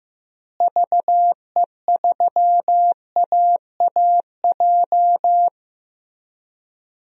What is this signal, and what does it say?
Morse code, heard as a keyed tone, about 15 wpm, 700 Hz. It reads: VE3AAJ
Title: DXCC-100 Call Signs - 6 Characters - Repeated 3x - 15wpm